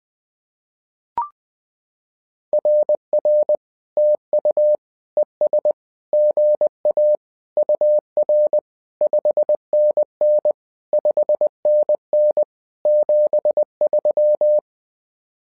Morse code, heard as a keyed tone, about 20 wpm, 600 Hz. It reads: RR TU ES GA UR 5NN 5NN 73
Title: SOTA QSO Practice 20wpm